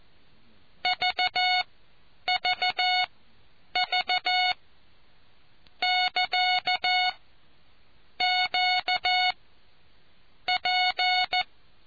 Morse code, heard as a keyed tone, about 14 wpm, spaced as for 8 wpm, 730 Hz. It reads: VVV <KA> QP